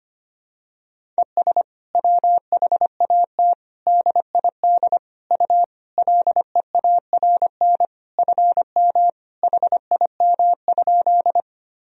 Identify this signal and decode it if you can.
Morse code, heard as a keyed tone, about 25 wpm, 700 Hz. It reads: ES WHAT DID U LEARN FM HIM?
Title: Sentences from Top 300 Words and Abbreviations 25wpm